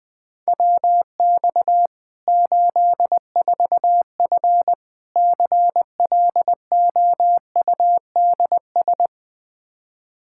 Morse code, heard as a keyed tone, about 21 wpm, 700 Hz. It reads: WX 84F CLOUDS
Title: Practice QSO - Exchange 2 - Info - 20wpm